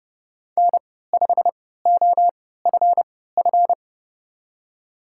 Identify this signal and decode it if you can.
Morse code, heard as a keyed tone, about 32 wpm, spaced as for 17 wpm, 700 Hz. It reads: N5OFF
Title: USA Call Signs - 5 Characters - 3x Character Spacing - 30wpm